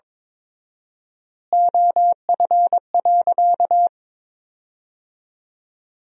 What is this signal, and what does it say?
Morse code, heard as a keyed tone, about 22 wpm, 700 Hz. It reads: OF.
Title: 4x Word-Spacing - Sentences from Top 300 Words - 22wpm